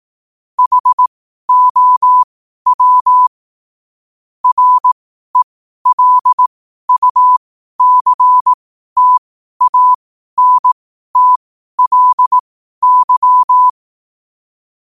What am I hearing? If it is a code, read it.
Morse code, heard as a keyed tone, about 18 wpm, spaced as for 12 wpm, 1000 Hz. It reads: HOW RELUCTANTLY
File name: qotdfiles / 12wpm